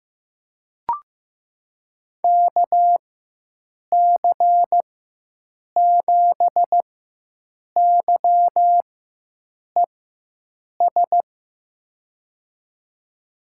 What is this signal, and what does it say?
Morse code, heard as a keyed tone, about 15 wpm, spaced as for 7 wpm, 700 Hz. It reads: KC7YES
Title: USA Call Signs - 6 Characters - 4x Character Spacing - 15wpm